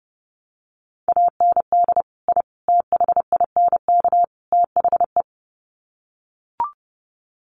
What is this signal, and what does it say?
Morse code, heard as a keyed tone, about 30 wpm, 700 Hz. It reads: AND I THINK THE
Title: Sets of 4 Words 30wpm